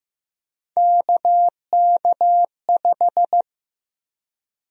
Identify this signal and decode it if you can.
Morse code, heard as a keyed tone, about 15 wpm, 700 Hz. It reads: KK5
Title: USA Call Sign Prefixes - Repeated 2x - 15wpm